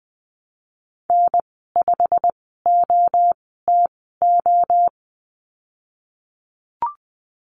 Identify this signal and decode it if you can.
Morse code, heard as a keyed tone, about 20 wpm, spaced as for 15 wpm, 700 Hz. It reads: N5OTO
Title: 154 Lesson 25g - Letter p - Call Signs 20wpm